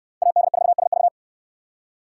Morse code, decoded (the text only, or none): AD4SV